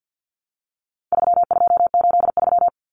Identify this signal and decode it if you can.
Morse code, heard as a keyed tone, about 50 wpm, 700 Hz. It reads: 3283
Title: Common Cut Numbers - 4-Digit 50wpm